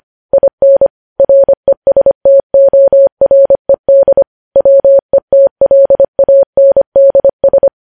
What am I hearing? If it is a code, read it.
Morse code, heard as a keyed tone, about 25 wpm, 550 Hz. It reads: IN RESTORED WETLANDS